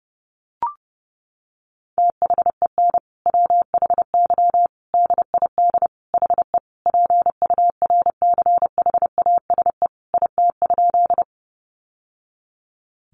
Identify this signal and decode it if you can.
Morse code, heard as a keyed tone, about 30 wpm, 700 Hz. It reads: THEN WHY DID HE PURCHASE IT?
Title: Sentences from Top 800 Words 30wpm